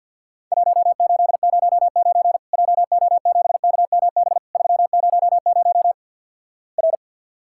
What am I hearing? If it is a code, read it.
Morse code, heard as a keyed tone, about 50 wpm, 700 Hz. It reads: 1809 JO7KMB 300 R
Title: Worked All Europe DX Contest - QTC Exchange - 50wpm